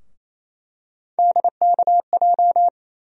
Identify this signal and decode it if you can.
Morse code, heard as a keyed tone, about 28 wpm, 700 Hz. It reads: DKJ
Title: USA Call Sign Postfixes - 3 Letters 28wpm